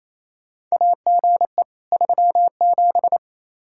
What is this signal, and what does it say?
Morse code, heard as a keyed tone, about 28 wpm, 700 Hz. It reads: AGE 37
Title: K1USN SST QSO Practice 28wpm